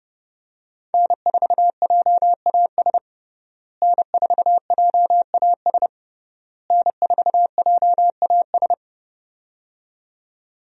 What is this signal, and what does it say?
Morse code, heard as a keyed tone, about 30 wpm, 700 Hz. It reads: N4JAS N4JAS N4JAS
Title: USA Call Signs - 5 Characters - Repeated 3x - 30wpm